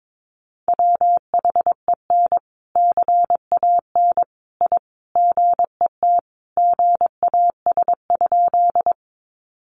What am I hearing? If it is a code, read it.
Morse code, heard as a keyed tone, about 22 wpm, 700 Hz. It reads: WHEN CAN I GET GAS?
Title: Sentences from Top 1100 Words 22wpm